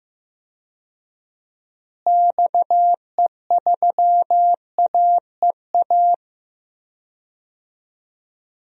Morse code, heard as a keyed tone, about 15 wpm, 700 Hz. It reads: XE3AEA
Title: DXCC-100 Call Signs - 6 Characters - Repeated 3x - 15wpm